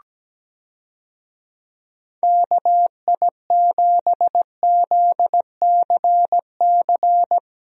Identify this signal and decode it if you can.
Morse code, heard as a keyed tone, about 17 wpm, 700 Hz. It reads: KI7ZCC
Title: USA Call Signs - 6 Characters - Repeated 2x - 17wpm